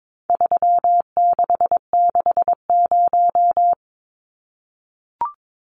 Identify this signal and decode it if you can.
Morse code, heard as a keyed tone, about 22 wpm, 700 Hz. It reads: 3660
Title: Numbers - 4-Digit 22wpm